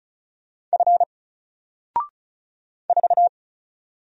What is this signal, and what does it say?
Morse code, heard as a keyed tone, about 36 wpm, 700 Hz. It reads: F 4
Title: Single Letter-Number Alt 35wpm